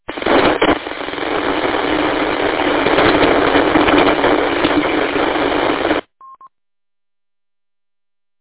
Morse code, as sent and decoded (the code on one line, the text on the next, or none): none